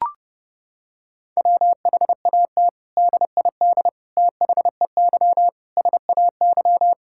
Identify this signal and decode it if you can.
Morse code, heard as a keyed tone, about 30 wpm, 700 Hz. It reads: WHAT DID THEY SAY
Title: Sentences from Top 1000 Words - Review Part 5 - 30wpm